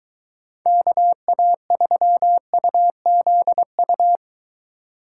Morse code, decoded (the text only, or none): KA3UZU